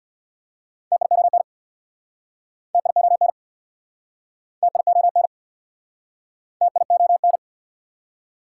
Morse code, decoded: TION TION TION TION

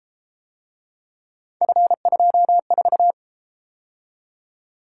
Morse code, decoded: F24

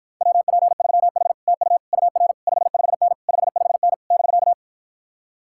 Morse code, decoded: WP3H TU UR 55N 55N <BK>

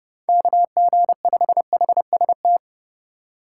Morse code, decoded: KG5HST

